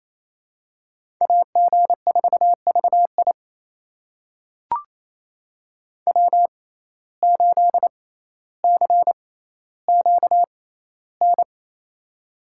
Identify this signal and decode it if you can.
Morse code, heard as a keyed tone, about 28 wpm, spaced as for 14 wpm, 700 Hz. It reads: AG4VI W8CQN
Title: USA Call Signs - 5 Characters - 6x Character Spacing - 28wpm